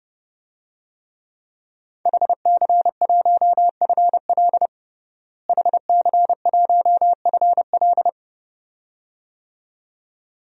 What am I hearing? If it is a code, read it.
Morse code, heard as a keyed tone, about 30 wpm, 700 Hz. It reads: HC1FL HC1FL
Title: DXCC-100 Call Signs - 5 Characters - Repeated 2x - 30wpm